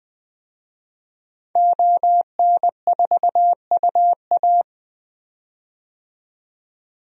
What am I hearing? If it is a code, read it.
Morse code, heard as a keyed tone, about 20 wpm, 700 Hz. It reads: ON4UA
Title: DXCC-100 Call Signs - 5 Characters - 20wpm